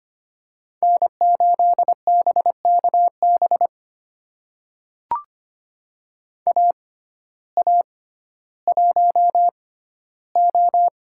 Morse code, heard as a keyed tone, about 25 wpm, spaced as for 14 wpm, 700 Hz. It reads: N8BKB AA1O
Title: USA Call Signs - 5 Characters - 6x Character Spacing - 25wpm